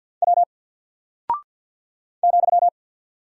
Morse code, heard as a keyed tone, about 50 wpm, 700 Hz. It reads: W ,
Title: Single Character (Letter:Number:Punctuation:Prosign) Alt 50wpm